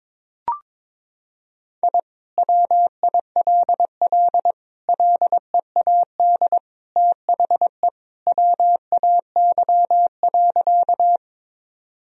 Morse code, heard as a keyed tone, about 21 wpm, 700 Hz. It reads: I WILL LEAD THE WAY.
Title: Sentences from Top 1000 Words - Review Part 2 - 22wpm